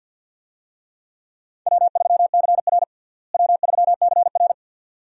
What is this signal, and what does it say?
Morse code, heard as a keyed tone, about 48 wpm, 690 Hz. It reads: W3CR W3CR
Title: USA Call Signs - 4 Characters - Repeated 2x - 50wpm